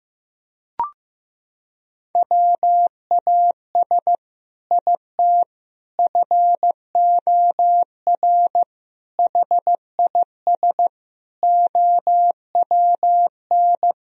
Morse code, heard as a keyed tone, about 14 wpm, 700 Hz. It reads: WAS IT FOR HIS OWN
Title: Sentences from Top 1000 Words - Review Part 3 - 15wpm